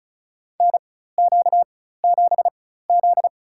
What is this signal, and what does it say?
Morse code, heard as a keyed tone, about 36 wpm, spaced as for 16 wpm, 700 Hz. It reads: NQ7Z
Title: USA Call Signs - 4x Character Spacing - 35wpm